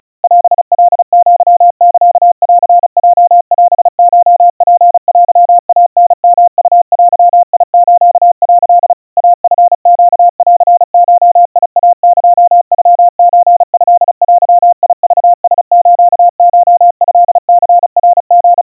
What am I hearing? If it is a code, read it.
Morse code (Japanese Wabun code), heard as a keyed tone, about 36 wpm, 700 Hz. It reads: オカアサンヲオコツテイタヨウデス」 イチネンゴイエノソトデクラスコトニナリ
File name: kawa / Morse sample / kamyu 130 cpm 700 Hz Crispy